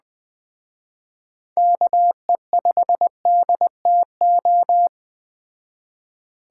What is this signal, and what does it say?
Morse code, heard as a keyed tone, about 21 wpm, 700 Hz. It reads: KE5DTO